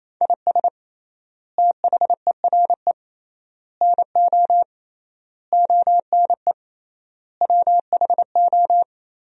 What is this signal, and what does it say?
Morse code, heard as a keyed tone, about 27 wpm, 700 Hz. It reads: IS THERE NO ONE WHO
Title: 3x Word-Spacing - Sentences from Top 200 Words - 28wpm